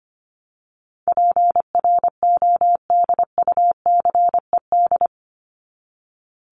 Morse code, heard as a keyed tone, about 25 wpm, 700 Hz. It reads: PRODUCED
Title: ED Suffix - Top 100 Words with Base Words 25wpm